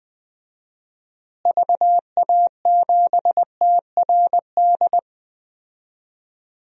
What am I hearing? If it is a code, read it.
Morse code, heard as a keyed tone, about 21 wpm, 700 Hz. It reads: VA7TRD
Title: DXCC-100 Call Signs - 6 Characters - Repeated 2x - 20wpm